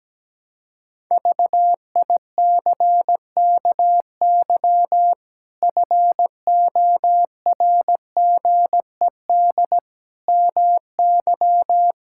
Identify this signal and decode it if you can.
Morse code, heard as a keyed tone, about 17 wpm, 700 Hz. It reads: VICKY FORGED MY
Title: Pangram Challenge v2 - 17wpm